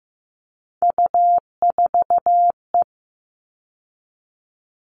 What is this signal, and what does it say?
Morse code, heard as a keyed tone, about 15 wpm, 700 Hz. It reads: U4E
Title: All Cut Numbers - 3-Digit 15wpm